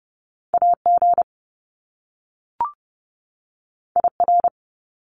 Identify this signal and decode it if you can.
Morse code, heard as a keyed tone, about 30 wpm, 700 Hz. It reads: AG IR
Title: Top Two Letter Combinations in Words 30wpm